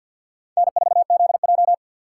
Morse code, heard as a keyed tone, about 53 wpm, 700 Hz. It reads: N4ZJ
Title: USA Call Signs - 4 Characters - 2x Character Spacing - 50wpm